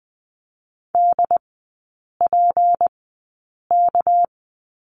Morse code, heard as a keyed tone, about 20 wpm, spaced as for 8 wpm, 690 Hz. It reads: DPK
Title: 212 Lesson 34e - Number 8 - Three Characters 20wpm